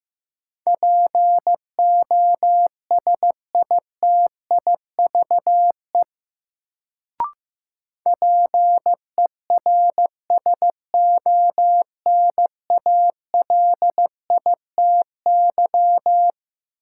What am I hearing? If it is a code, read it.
Morse code, heard as a keyed tone, about 15 wpm, 700 Hz. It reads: POSITIVE PERSONALITY
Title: Sentences from Top 1400 Words - New Word Drill Alt - 15wpm